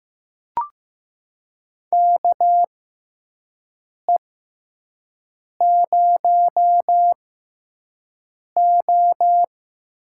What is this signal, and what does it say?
Morse code, heard as a keyed tone, about 15 wpm, spaced as for 5 wpm, 700 Hz. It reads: KE0O
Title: USA Call Signs - 6x Character Spacing - 15wpm